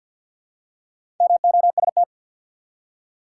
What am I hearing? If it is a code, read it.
Morse code, heard as a keyed tone, about 53 wpm, 700 Hz. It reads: MOST
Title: Four Letter Words - 50wpm